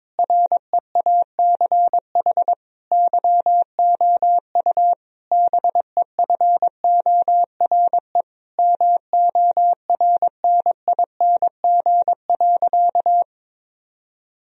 Morse code, heard as a keyed tone, about 22 wpm, 700 Hz. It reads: REACH YOU BEFORE MORNING.
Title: Sentences from Top 1000 Words - Review Part 2 - 22wpm